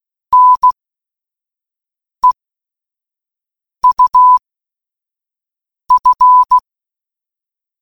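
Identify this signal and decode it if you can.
Morse code, heard as a keyed tone, about 16 wpm, spaced as for 5 wpm, 1000 Hz. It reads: NEUF
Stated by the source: Morse code puzzle in an escape game: three coded words giving a padlock's three digits.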